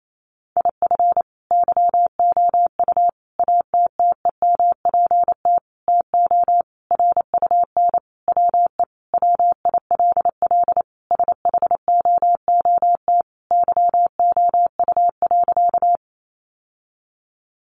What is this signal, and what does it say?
Morse code, heard as a keyed tone, about 28 wpm, 700 Hz. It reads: IF YOU ATTEMPT TO RUN WE WILL SHOOT YOU.
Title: Sentences from Top 1200 Words 28wpm